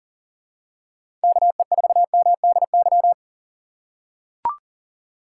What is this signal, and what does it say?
Morse code, heard as a keyed tone, about 40 wpm, 700 Hz. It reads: KE4MDY
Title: USA Call Signs - 6 Characters - 3x Character Spacing - 40wpm